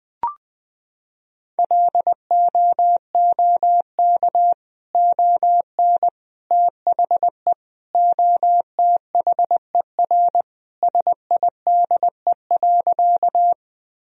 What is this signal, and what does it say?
Morse code, heard as a keyed tone, about 19 wpm, 700 Hz. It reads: LOOK ON THE OTHER SIDE.
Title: Sentences from Top 1000 Words - Review Part 1 - 20wpm